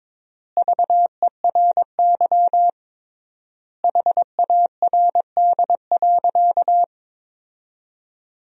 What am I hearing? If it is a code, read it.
Morse code, heard as a keyed tone, about 21 wpm, 700 Hz. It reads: VERY HARD.
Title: 3x Word-Spacing - Sentences from Top 300 Words - 22wpm